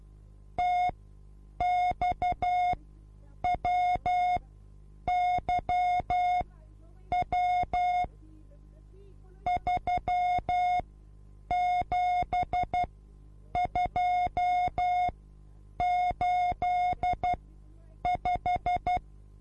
Morse code, heard as a keyed tone, about 12 wpm, spaced as for 8 wpm, 720 Hz. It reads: TXWYW 37285
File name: repos / Morsekurs / DJ4UF Morsen24